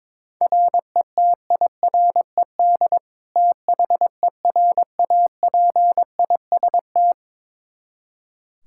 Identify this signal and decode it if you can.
Morse code, heard as a keyed tone, about 22 wpm, 700 Hz. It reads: RETIRED THERAPIST
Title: Practice QSO - Exchange 3 - More Info - 22wpm